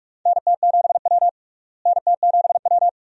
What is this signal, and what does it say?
Morse code, heard as a keyed tone, about 46 wpm, 700 Hz. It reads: NT7W NT7W
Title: USA Call Signs - 4 Characters - Repeated 2x - 45wpm